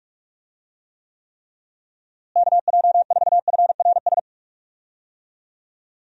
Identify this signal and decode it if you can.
Morse code, heard as a keyed tone, about 41 wpm, 700 Hz. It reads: KJ4FRS
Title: USA Call Signs - 6 Characters - Repeated 3x - 45wpm